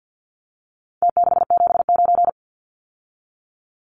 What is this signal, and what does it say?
Morse code, heard as a keyed tone, about 50 wpm, 700 Hz. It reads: T679
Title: Numbers - Contest Style - 4-Digit 50wpm